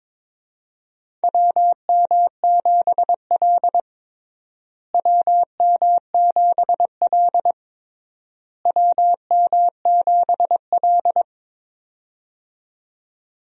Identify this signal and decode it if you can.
Morse code, heard as a keyed tone, about 22 wpm, 700 Hz. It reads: WM7L WM7L WM7L
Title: USA Call Signs - 4 Characters - Repeated 3x - 22wpm